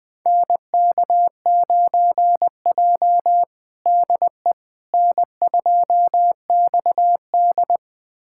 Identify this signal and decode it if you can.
Morse code, heard as a keyed tone, about 20 wpm, 700 Hz. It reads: NK9J DE N2XD
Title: USA Callsign DE Callsign - 4 Characters - 20wpm